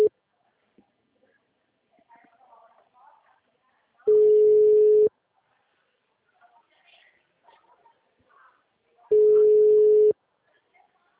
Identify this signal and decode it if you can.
Ringback tone (Europe, 425 Hz), with speech.
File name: out- -2008- - - .31426